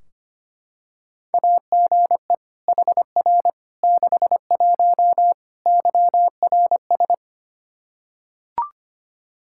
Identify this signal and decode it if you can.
Morse code, heard as a keyed tone, about 25 wpm, 700 Hz. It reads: AGE HR 61 YRS
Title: Practice QSO - Exchange 3 - More Info - 25wpm